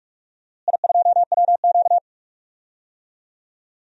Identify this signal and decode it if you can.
Morse code, heard as a keyed tone, about 47 wpm, 690 Hz. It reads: I2WQ
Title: DXCC-100 Call Signs - 4 Characters - Repeated 3x - 45wpm